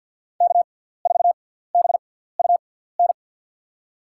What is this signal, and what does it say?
Morse code, heard as a keyed tone, about 47 wpm, spaced as for 17 wpm, 700 Hz. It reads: K4BUN
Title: USA Call Signs - 6x Character Spacing - 50wpm